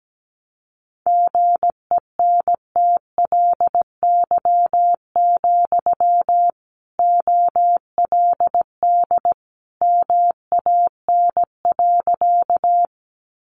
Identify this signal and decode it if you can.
Morse code, heard as a keyed tone, about 17 wpm, 700 Hz. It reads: GENTLY, OLD MAN.
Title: Sentences from Top 1500 Words 17wpm